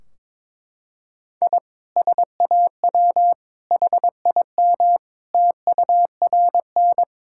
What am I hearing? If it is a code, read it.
Morse code, heard as a keyed tone, about 22 wpm, 700 Hz. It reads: I SAW HIM TURN
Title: Sentences from Top 1600 Words 22wpm